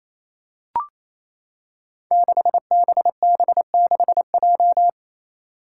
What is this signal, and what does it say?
Morse code, heard as a keyed tone, about 28 wpm, 700 Hz. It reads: 6BB6J
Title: Five Challenging Letter-Number Combinations 28wpm